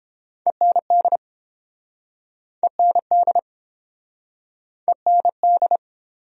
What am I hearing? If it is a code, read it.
Morse code, heard as a keyed tone, about 30 wpm, 700 Hz. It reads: END END END